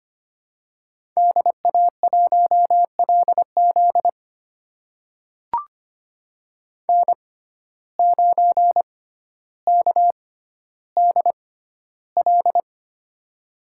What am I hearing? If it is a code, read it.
Morse code, heard as a keyed tone, about 25 wpm, spaced as for 13 wpm, 700 Hz. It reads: DA1LZ N9KDL
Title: USA Call Signs - 5 Characters - 6x Character Spacing - 25wpm